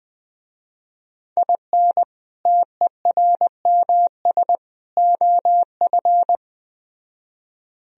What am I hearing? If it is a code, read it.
Morse code, heard as a keyed tone, about 20 wpm, 700 Hz. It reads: IN TERMS OF
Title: Speed-Racing - Sets of 3 Words 17wpm